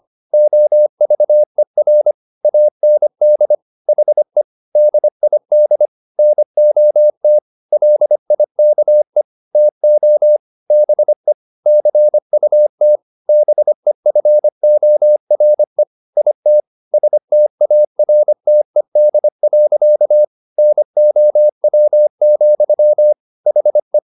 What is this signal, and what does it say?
Morse code, heard as a keyed tone, about 25 wpm, 600 Hz. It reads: OVER AND HE DID NOT LIKE TO BE CUT BEFORE IT STARTED. NOW, HE
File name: the old man and the sea part 1 25wpm